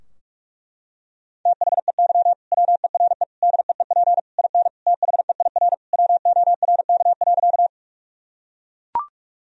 Morse code, heard as a keyed tone, about 41 wpm, 700 Hz. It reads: THEY WERE DEEP IN THEIR WORK.
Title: Sentences from Top 1000 Words - Review Part 3 - 45wpm